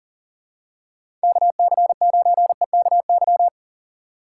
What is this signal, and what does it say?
Morse code, heard as a keyed tone, about 40 wpm, 700 Hz. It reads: KC9EKY
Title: USA Call Signs - 6 Characters - 3x Character Spacing - 40wpm